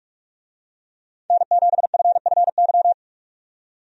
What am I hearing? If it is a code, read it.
Morse code, heard as a keyed tone, about 48 wpm, 700 Hz. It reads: N7FFY
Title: USA Call Signs - 5 Characters - 45wpm